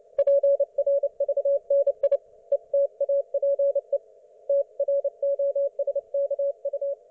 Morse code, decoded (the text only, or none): PRVNI ETAPE TROSKU